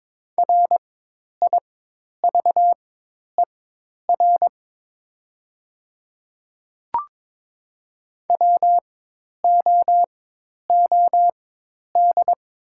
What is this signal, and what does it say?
Morse code, heard as a keyed tone, about 22 wpm, spaced as for 10 wpm, 700 Hz. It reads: RIVER WOOD
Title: 4x Character Spacing - Top 300 Words - 22wpm